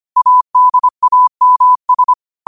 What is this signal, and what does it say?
Morse code, heard as a keyed tone, about 25 wpm, 1000 Hz. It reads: ADAMS
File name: 25wpm